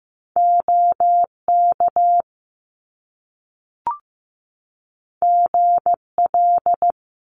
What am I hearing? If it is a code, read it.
Morse code, heard as a keyed tone, about 15 wpm, 700 Hz. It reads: OK GL
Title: Top Two Letter Combinations in Words 15wpm